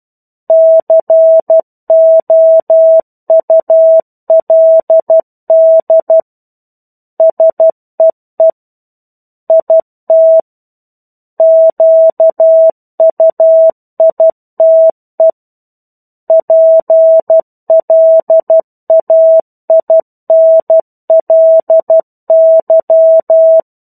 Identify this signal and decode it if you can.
Morse code, heard as a keyed tone, about 12 wpm, 650 Hz. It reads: COULD SEE IT QUITE PLAINLY
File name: 01-Chapter-01 12 WPM